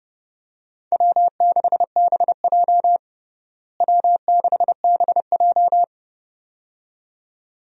Morse code, heard as a keyed tone, about 30 wpm, 700 Hz. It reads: W6BJ W6BJ
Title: USA Call Signs - 4 Characters - Repeated 2x - 30wpm